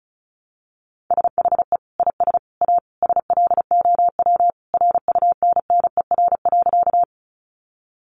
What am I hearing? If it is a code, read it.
Morse code, heard as a keyed tone, about 34 wpm, 700 Hz. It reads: SHE IS A SLOW RUNNER.
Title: Additional Easy Phrases 35wpm